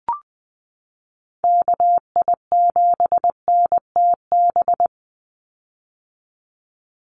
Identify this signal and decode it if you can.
Morse code, heard as a keyed tone, about 20 wpm, 700 Hz. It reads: KI7NTB
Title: USA Call Signs 20wpm